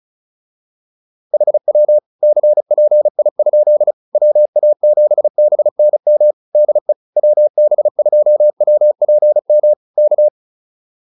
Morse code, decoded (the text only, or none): HW CPI? WA7BNM DE WB2WPM K